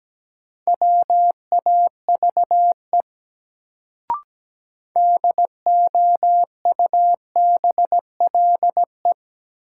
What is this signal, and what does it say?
Morse code, heard as a keyed tone, about 17 wpm, 700 Hz. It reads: WAVE DOUBLE
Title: Sentences from Top 1400 Words - New Word Drill - 17wpm